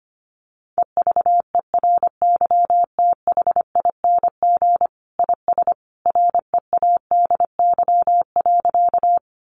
Morse code, heard as a keyed tone, about 25 wpm, 700 Hz. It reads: EVERYTHING IS READY.
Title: Sentences from Top 700 Words 25wpm